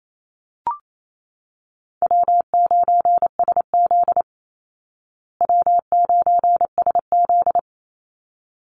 Morse code, heard as a keyed tone, about 28 wpm, 700 Hz. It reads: W9SZ W9SZ
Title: Call Signs Repeated 3x - 28wpm